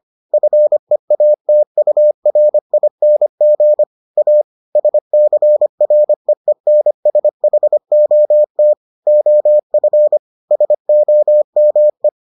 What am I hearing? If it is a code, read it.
Morse code, heard as a keyed tone, about 25 wpm, 600 Hz. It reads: FEATURING A SCREENSHOT OF SOME